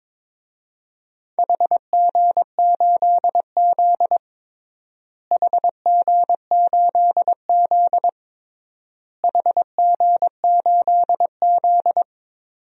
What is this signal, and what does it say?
Morse code, heard as a keyed tone, about 22 wpm, 700 Hz. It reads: HG8Z HG8Z HG8Z
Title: DXCC-100 Call Signs - 4 Characters - Repeated 3x - 22wpm